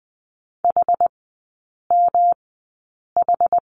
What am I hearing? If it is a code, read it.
Morse code, heard as a keyed tone, about 20 wpm, spaced as for 8 wpm, 700 Hz. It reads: HMH
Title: 107 Lesson 18e - Letter m - Three Characters 20wpm